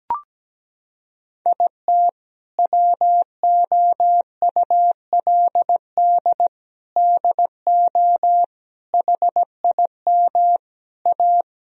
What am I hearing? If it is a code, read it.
Morse code, heard as a keyed tone, about 17 wpm, 700 Hz. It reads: IT WOULD DO HIM A
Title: Sentences from Top 1000 Words - Review Part 3 - 17wpm